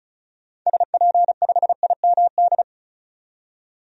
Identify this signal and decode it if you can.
Morse code, heard as a keyed tone, about 34 wpm, 700 Hz. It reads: SP5IMD
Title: DXCC-100 Call Signs - 6 Characters - 35wpm